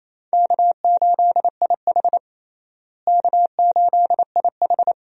Morse code, heard as a keyed tone, about 28 wpm, 700 Hz. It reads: K8IH K8IH
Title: USA Call Signs - 4 Characters - Repeated 2x - 28wpm